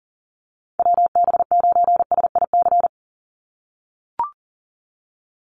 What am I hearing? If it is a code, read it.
Morse code, heard as a keyed tone, about 41 wpm, 700 Hz. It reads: WB9SIC
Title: USA Call Signs 40wpm